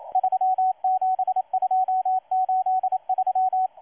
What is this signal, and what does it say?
Morse code, heard as a keyed tone, about 27 wpm, 750 Hz, with radio noise. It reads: 37283
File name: M12-2016-07-07-1310utc-13926khz-error-restart